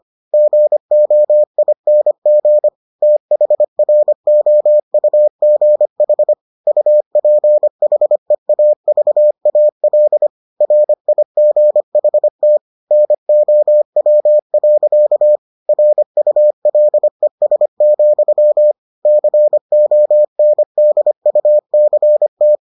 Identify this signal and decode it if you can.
Morse code, heard as a keyed tone, about 25 wpm, 600 Hz. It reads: GOING THROUGH UPHEAVAL RIGHT NOW. RULES, CONDUCT